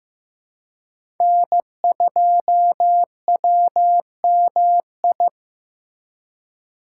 Morse code, heard as a keyed tone, about 16 wpm, 700 Hz. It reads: N2WMI
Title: USA Call Signs - 5 Characters - 4x Character Spacing - 15wpm